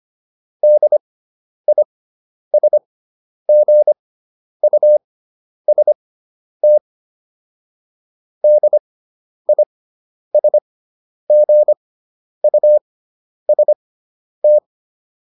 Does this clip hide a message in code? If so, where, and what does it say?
Yes: Morse code from 0.6 to 14.6 s, heard as a keyed tone, about 25 wpm, spaced as for 10 wpm, 600 Hz. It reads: DISGUST DISGUST